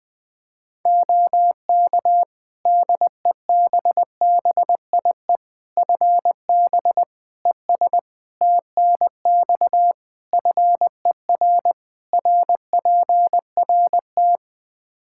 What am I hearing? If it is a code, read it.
Morse code, heard as a keyed tone, about 20 wpm, 700 Hz. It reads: OK DEBBIE FB ES TNX FER RPRT
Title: Practice QSO - Exchange 2 - Info - 20wpm